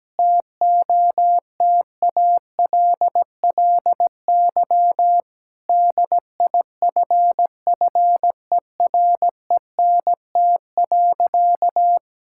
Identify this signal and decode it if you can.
Morse code, heard as a keyed tone, about 17 wpm, 700 Hz. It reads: TOTALLY DIFFERENT.
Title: Sentences from Top 1600 Words 17wpm